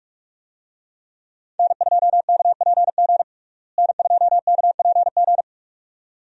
Morse code, heard as a keyed tone, about 41 wpm, 700 Hz. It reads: N2KPG N2KPG
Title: USA Call Signs - 5 Characters - Repeated 2x - 45wpm